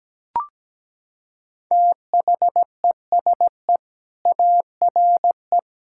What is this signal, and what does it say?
Morse code, heard as a keyed tone, about 17 wpm, 700 Hz. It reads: THESE ARE
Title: Sentences from Top 1000 Words - Review Part 5 - 17wpm